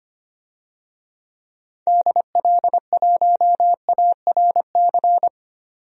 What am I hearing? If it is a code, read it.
Morse code, heard as a keyed tone, about 25 wpm, 700 Hz. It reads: DL1ARC
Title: DXCC-100 Call Signs - 6 Characters - 25wpm